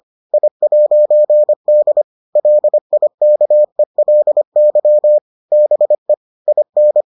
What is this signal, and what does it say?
Morse code, heard as a keyed tone, about 25 wpm, 600 Hz. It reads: I'D LIKELY BE IN